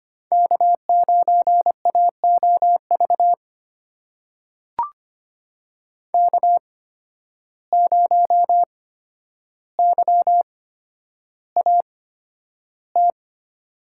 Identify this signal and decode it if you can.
Morse code, heard as a keyed tone, about 25 wpm, spaced as for 11 wpm, 700 Hz. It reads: K9AOV K0YAT
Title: USA Call Signs - 5 Characters - 8x Character Spacing - 25wpm